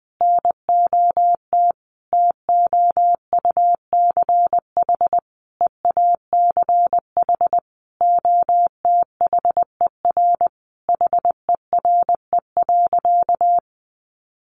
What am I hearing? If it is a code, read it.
Morse code, heard as a keyed tone, about 20 wpm, 700 Hz. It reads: NOT TOUCH EACH OTHER HERE.
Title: Sentences from Top 1200 Words 20wpm